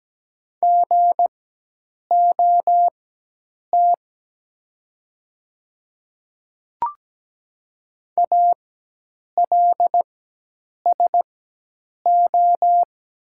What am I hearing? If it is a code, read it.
Morse code, heard as a keyed tone, about 17 wpm, spaced as for 8 wpm, 700 Hz. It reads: GOT ALSO
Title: 4x Character Spacing - Top 500 Words - 17wpm